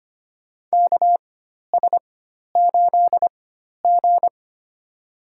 Morse code, heard as a keyed tone, about 25 wpm, spaced as for 12 wpm, 700 Hz. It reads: KS8G
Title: USA Call Signs - 4x Character Spacing - 25wpm